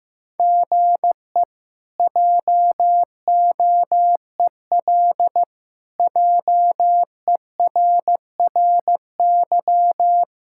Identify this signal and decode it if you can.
Morse code, heard as a keyed tone, about 16 wpm, 700 Hz. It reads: GE JOEL JERRY
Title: K1USN SST QSO Practice - 15wpm